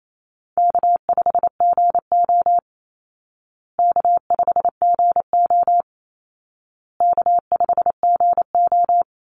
Morse code, heard as a keyed tone, about 28 wpm, 700 Hz. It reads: K5GO K5GO K5GO
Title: Easy Call Signs Repeated 3x - 28wpm